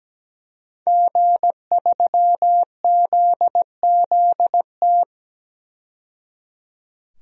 Morse code, heard as a keyed tone, about 17 wpm, 700 Hz. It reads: G3ZZT
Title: DXCC-100 Call Signs - 5 Characters - 17wpm